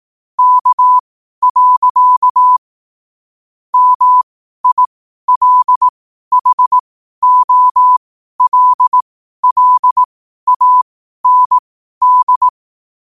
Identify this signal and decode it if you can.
Morse code, heard as a keyed tone, about 18 wpm, spaced as for 12 wpm, 1000 Hz. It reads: K. MILHOLLAND
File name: qotdfiles / 12wpm